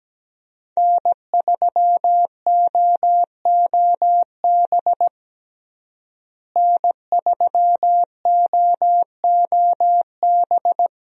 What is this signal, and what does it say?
Morse code, heard as a keyed tone, about 17 wpm, 700 Hz. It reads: N3OOB N3OOB
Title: USA Call Signs - 5 Characters - Repeated 3x - 17wpm